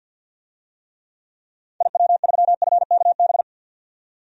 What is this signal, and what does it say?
Morse code, heard as a keyed tone, about 48 wpm, 700 Hz. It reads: IW3FKB
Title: DXCC-100 Call Signs - 6 Characters - 50wpm